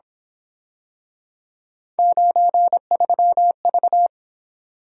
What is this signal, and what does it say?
Morse code, heard as a keyed tone, about 26 wpm, 700 Hz. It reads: 93V
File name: Speed-Racing - Three Letter-Number Combinations - 22wpm